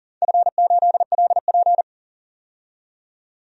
Morse code, heard as a keyed tone, about 38 wpm, 700 Hz. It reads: F8LP
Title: DXCC-100 Call Signs - 4 Characters - 40wpm